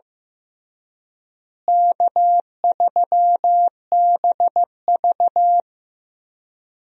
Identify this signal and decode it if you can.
Morse code, heard as a keyed tone, about 16 wpm, 700 Hz. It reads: K3BV
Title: USA Call Signs - 4 Characters - Repeated 3x - 15wpm